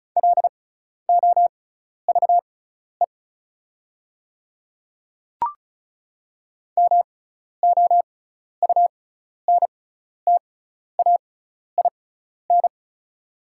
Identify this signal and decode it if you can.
Morse code, heard as a keyed tone, about 34 wpm, spaced as for 12 wpm, 690 Hz. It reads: LOVE MOUNTAIN
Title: 6x Character Spacing - Top 500 Words - 35wpm